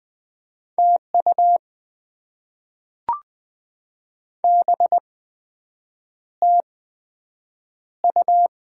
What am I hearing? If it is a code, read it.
Morse code, heard as a keyed tone, about 20 wpm, spaced as for 7 wpm, 700 Hz. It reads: TU BTU
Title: Top 20 QSO Elements - 8x Character Spacing - 20wpm